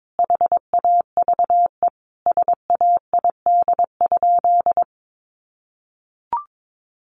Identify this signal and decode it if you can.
Morse code, heard as a keyed tone, about 22 wpm, 700 Hz. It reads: HAVE SAID?
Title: Sentences from Top 100 Words 22wpm